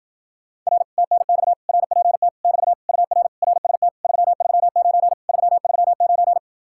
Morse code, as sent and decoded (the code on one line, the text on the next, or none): .-. / - -. -..- / .-. .--. - / -...- / ..- .-. / .-. ... - / ...-- ...-- ----. / ...-- ...-- ----.
R TNX RPT <BT> UR RST 339 339